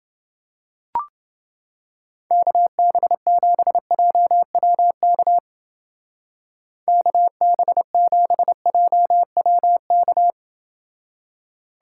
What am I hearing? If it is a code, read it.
Morse code, heard as a keyed tone, about 29 wpm, 700 Hz. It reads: KB7JWK KB7JWK